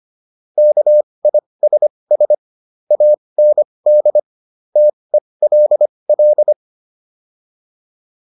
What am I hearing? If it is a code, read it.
Morse code, heard as a keyed tone, about 25 wpm, spaced as for 20 wpm, 600 Hz. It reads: KISS AND TELL
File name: PR304 20